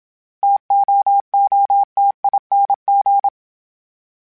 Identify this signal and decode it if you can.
Morse code, heard as a keyed tone, about 27 wpm, 800 Hz. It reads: TOOTING
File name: Difficult Combos / Difficult Combos-mp3s / T AOEI @25